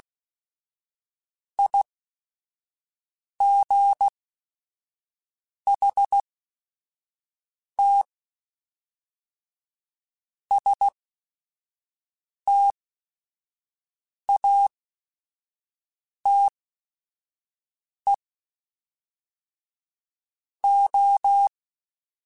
Morse code, heard as a keyed tone, about 16 wpm, spaced as for 5 wpm, 800 Hz. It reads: IGHT STATE O
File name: Technical Subjects / Technical Subjects-mp3s / OR Gate @5